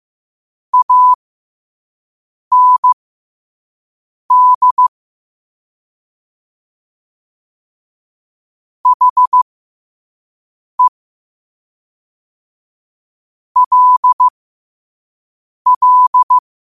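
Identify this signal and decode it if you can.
Morse code, heard as a keyed tone, about 15 wpm, spaced as for 5 wpm, 1000 Hz. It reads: AND HE LL
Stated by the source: Morse code quote of the day sent at 5 wpm.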